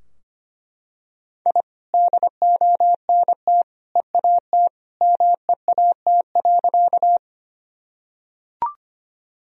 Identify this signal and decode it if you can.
Morse code, heard as a keyed tone, about 25 wpm, 700 Hz. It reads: I DONT EAT MEAT.
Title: Sentences from Top 1000 Words - Review Part 3 - 25wpm